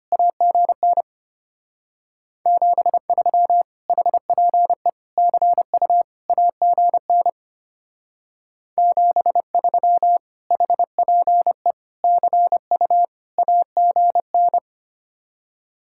Morse code, heard as a keyed tone, about 31 wpm, 700 Hz. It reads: AGN 73 HPE CU AGN 73 HPE CU AGN
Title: Speed-Racing Common QSO Phrases - 25wpm